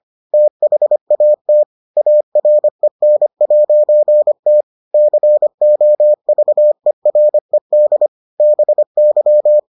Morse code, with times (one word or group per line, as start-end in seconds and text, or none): 0.3-1.6 THAT
2.0-4.6 AREN'T
4.9-8.1 COVERED
8.4-9.6 BY